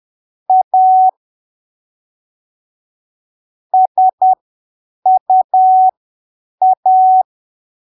Morse code, heard as a keyed tone, about 10 wpm, spaced as for 7 wpm, 750 Hz. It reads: A SUA
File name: arquivos / texto / texto 28 10ppm